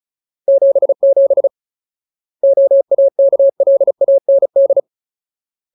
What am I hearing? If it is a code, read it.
Morse code, heard as a keyed tone, about 34 wpm, 550 Hz. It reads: 77 OAKLAND
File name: Ultimate Word List / 35WPM Ultimate